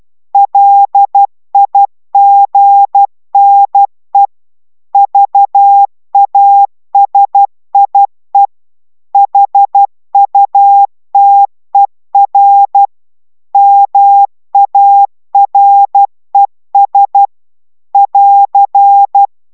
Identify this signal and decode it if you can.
Morse code, heard as a keyed tone, about 12 wpm, 800 Hz. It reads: LIGNE VASIE HUTER MARES <AR>